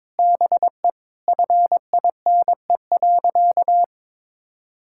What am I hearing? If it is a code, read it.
Morse code, heard as a keyed tone, about 22 wpm, 700 Hz. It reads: BE FINE.
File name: Sentences with 7-Word Phrases - 22wpm